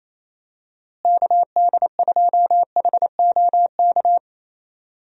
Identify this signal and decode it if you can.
Morse code, heard as a keyed tone, about 27 wpm, 700 Hz. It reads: KD2HOK
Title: USA Call Signs - 6 Characters - Repeated 2x - 28wpm